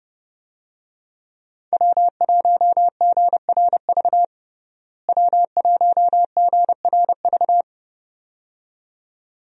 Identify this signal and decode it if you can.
Morse code, heard as a keyed tone, about 30 wpm, 700 Hz. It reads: W1GRV W1GRV
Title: USA Call Signs - 5 Characters - Repeated 2x - 30wpm